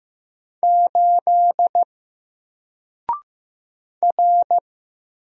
Morse code, heard as a keyed tone, about 15 wpm, 700 Hz. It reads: 8 R
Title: Single Character (Letter:Number:Punctuation:Prosign) Alt 15wpm